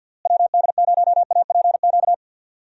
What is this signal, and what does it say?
Morse code, heard as a keyed tone, about 49 wpm, 700 Hz. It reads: WD0APQ
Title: USA Call Signs - 6 Characters - Repeated 3x - 50wpm